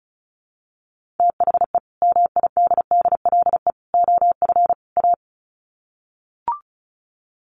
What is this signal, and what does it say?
Morse code, heard as a keyed tone, about 34 wpm, 700 Hz. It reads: THE MIDDLE OF A
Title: Sets of 4 Words 35wpm